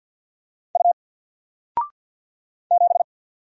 Morse code, heard as a keyed tone, about 54 wpm, 700 Hz. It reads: U 7
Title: Single Letter-Number Alt 50wpm